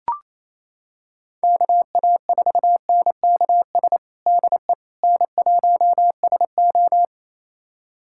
Morse code, heard as a keyed tone, about 28 wpm, 700 Hz. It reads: KA4NKS DE N1SO
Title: USA Callsign DE Callsign 28wpm